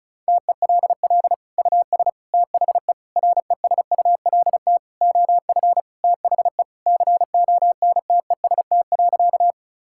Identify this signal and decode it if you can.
Morse code, heard as a keyed tone, about 36 wpm, 700 Hz. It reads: TELL US THE RESULT OF THE CONTEST.
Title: Sentences from Top 1600 Words 35wpm